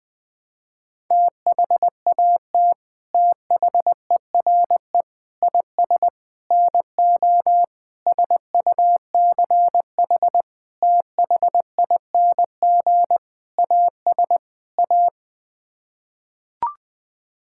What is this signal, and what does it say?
Morse code, heard as a keyed tone, about 20 wpm, 700 Hz. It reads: THAT THERE IS NO SUCH THING AS A
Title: Sets of 8 Words - 20wpm